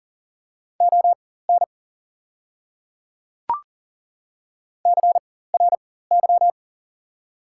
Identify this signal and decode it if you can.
Morse code, heard as a keyed tone, about 38 wpm, spaced as for 18 wpm, 700 Hz. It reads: ON CRY
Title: 4x Character Spacing - Top 500 Words - 40wpm